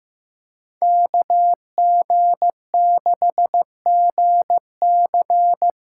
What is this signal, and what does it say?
Morse code, heard as a keyed tone, about 15 wpm, 700 Hz. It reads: KG6GC
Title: USA Call Signs - 6 Characters - 8x Character Spacing - 15wpm